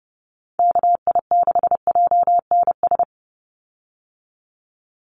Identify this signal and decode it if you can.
Morse code, heard as a keyed tone, about 30 wpm, 700 Hz. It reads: KI6JNS
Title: USA Call Signs 30wpm